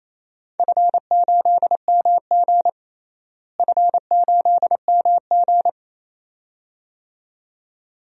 Morse code, heard as a keyed tone, about 28 wpm, 700 Hz. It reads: F8MG F8MG
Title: DXCC-100 Call Signs - 4 Characters - Repeated 2x - 28wpm